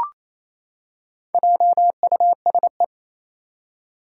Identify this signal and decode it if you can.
Morse code, heard as a keyed tone, about 27 wpm, 700 Hz. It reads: JUSE
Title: Four Challenging Letter-Number Combinations 28wpm